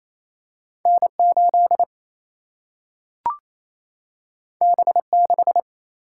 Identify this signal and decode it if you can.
Morse code, heard as a keyed tone, about 28 wpm, 700 Hz. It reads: N 8 B 6
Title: Single-Letter Single-Number 28wpm